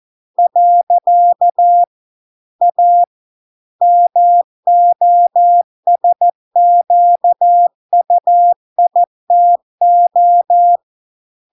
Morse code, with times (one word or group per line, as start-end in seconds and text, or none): 0.4-1.8 .
2.6-3.0 A
3.8-10.8 MOSQUITO